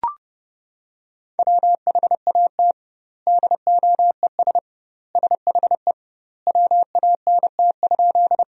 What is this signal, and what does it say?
Morse code, heard as a keyed tone, about 30 wpm, 700 Hz. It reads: WHAT DOES SHE WANT?
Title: 2x Word-Spacing - Sentences from Top 200 Words - 30wpm